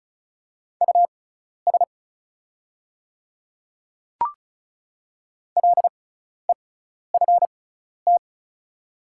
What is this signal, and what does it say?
Morse code, heard as a keyed tone, about 34 wpm, spaced as for 12 wpm, 690 Hz. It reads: US LEFT
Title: 6x Character Spacing - Top 300 Words - 35wpm